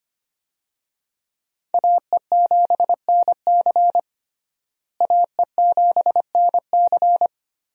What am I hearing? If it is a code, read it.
Morse code, heard as a keyed tone, about 25 wpm, 700 Hz. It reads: AE7NC AE7NC
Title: USA Call Signs - 5 Characters - Repeated 2x - 25wpm